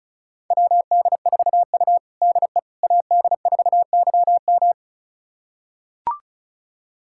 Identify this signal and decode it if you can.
Morse code, heard as a keyed tone, about 34 wpm, 700 Hz. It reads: WD4U DE AD4YM
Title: USA Callsign DE Callsign 35wpm